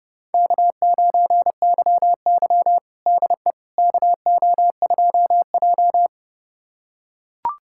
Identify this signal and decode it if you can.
Morse code, heard as a keyed tone, about 32 wpm, 700 Hz. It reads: K9YY DE KO2J
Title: USA Callsign DE Callsign - 4 Characters - 30wpm